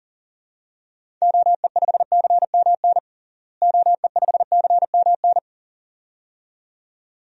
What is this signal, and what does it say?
Morse code, heard as a keyed tone, about 40 wpm, 700 Hz. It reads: OE5CMN OE5CMN
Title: DXCC-100 Call Signs - 6 Characters - Repeated 2x - 40wpm